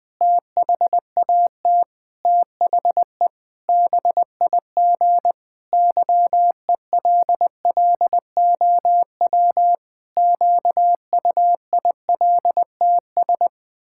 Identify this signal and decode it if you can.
Morse code, heard as a keyed tone, about 20 wpm, 700 Hz. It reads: THAT THE BIG YELLOW QUILTS
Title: Pangram Challenge v2 - 20wpm